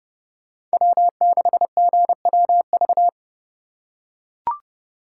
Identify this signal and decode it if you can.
Morse code, heard as a keyed tone, about 28 wpm, 700 Hz. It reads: W6GWV